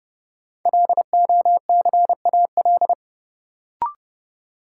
Morse code, heard as a keyed tone, about 30 wpm, 700 Hz. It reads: LOCAL
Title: Sentences from Top 1500 Words - New Word Drill Alt - 30wpm